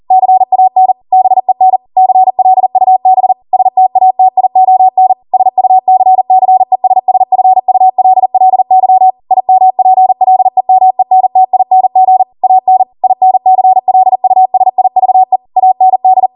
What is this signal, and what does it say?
Morse code, heard as a keyed tone, about 40 wpm, 750 Hz. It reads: CAN BEN CLUB STATION SUCCESSFULLY IMPLEMENTING AN INCLUSIVE AND